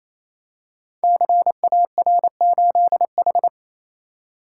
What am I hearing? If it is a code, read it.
Morse code, heard as a keyed tone, about 29 wpm, 690 Hz. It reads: CAR8H
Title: Code Groups - 5 Letter:Numbers - 28wpm